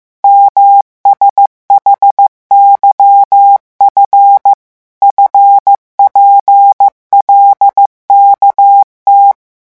Morse code, heard as a keyed tone, about 15 wpm, 800 Hz. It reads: MSHYF FPLKT
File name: morse mp3 / revision7-vitesse 14 mots